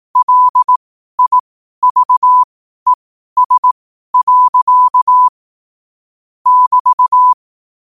Morse code, heard as a keyed tone, about 18 wpm, spaced as for 12 wpm, 1000 Hz. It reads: LIVES. <BT>
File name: qotdfiles / 12wpm